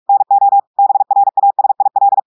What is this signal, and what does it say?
Morse code, heard as a keyed tone, about 46 wpm, 800 Hz. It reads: NO BRASIL
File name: audios / portugais / portugais texte en clair 1 vitesse 45 mots minutes